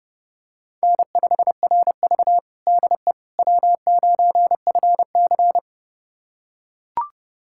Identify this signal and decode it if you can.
Morse code, heard as a keyed tone, about 31 wpm, 700 Hz. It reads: N5RV DE W9FC
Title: USA Callsign DE Callsign - 4 Characters - 30wpm